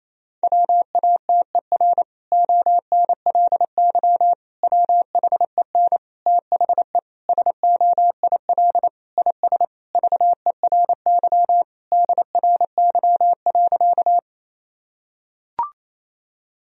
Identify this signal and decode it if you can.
Morse code, heard as a keyed tone, about 28 wpm, 700 Hz. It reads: WATER ONLY WHEN THE SOIL IS VERY DRY.
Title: Sentences from Top 1000 Words - Review Part 4 - 28wpm